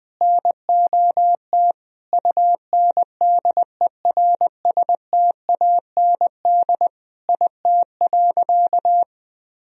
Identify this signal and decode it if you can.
Morse code, heard as a keyed tone, about 21 wpm, 700 Hz. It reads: NOT UNDERSTAND IT.
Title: Sentences from Top 1000 Words - Review Part 3 - 20wpm